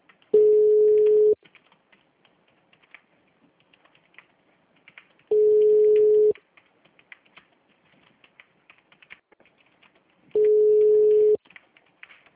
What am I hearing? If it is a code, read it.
Ringback tone (Europe, 425 Hz).